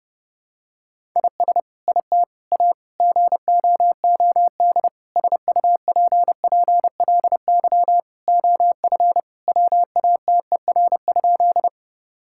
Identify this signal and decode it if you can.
Morse code, heard as a keyed tone, about 30 wpm, 700 Hz. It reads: IS IT A GOOD SUPPLY OF WATER?
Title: Sentences from Top 1000 Words - Review Part 5 - 30wpm